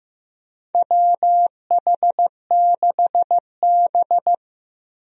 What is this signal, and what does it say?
Morse code, heard as a keyed tone, about 15 wpm, 700 Hz. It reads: WH6B
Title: USA Call Signs - 4 Characters - Repeated 2x - 15wpm